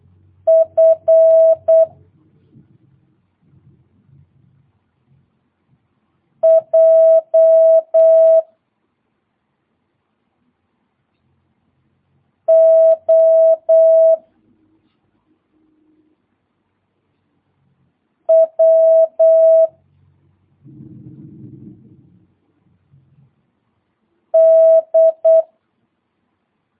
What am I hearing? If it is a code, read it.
Morse code, heard as a keyed tone, about 8 wpm, 650 Hz. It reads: F J O W D
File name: jh8cbh / 2022cw